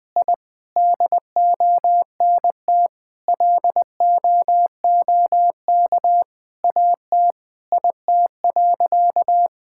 Morse code, heard as a keyed tone, about 20 wpm, 700 Hz. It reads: I DONT LOOK AT IT.
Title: Sentences from Top 1000 Words - Review Part 5 - 20wpm